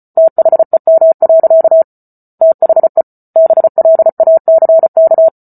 Morse code, heard as a keyed tone, about 34 wpm, 650 Hz. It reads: THEM. THE BLACK